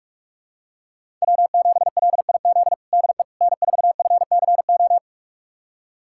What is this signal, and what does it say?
Morse code, heard as a keyed tone, about 46 wpm, 700 Hz. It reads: W7LIZ DE N4FCO
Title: USA Callsign DE Callsign - 5 Characters - 45wpm